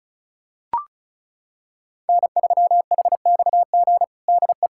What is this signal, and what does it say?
Morse code, heard as a keyed tone, about 36 wpm, 700 Hz. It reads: N3HXG DE
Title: K1USN SST QSO Practice 35wpm